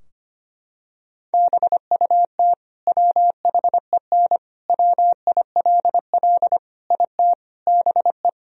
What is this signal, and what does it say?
Morse code, heard as a keyed tone, about 25 wpm, 700 Hz. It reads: BUT WHEN WILL IT BE
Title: Sentences from Top 1000 Words - Review Part 3 - 25wpm